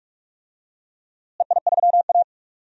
Morse code, heard as a keyed tone, about 45 wpm, 700 Hz. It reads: EI3A
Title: DXCC-100 Call Signs - 4 Characters - Repeated 3x - 45wpm